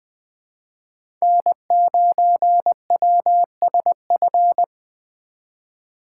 Morse code, heard as a keyed tone, about 20 wpm, 700 Hz. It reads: N9WSF
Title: USA Call Signs - 5 Characters - 3x Character Spacing - 20wpm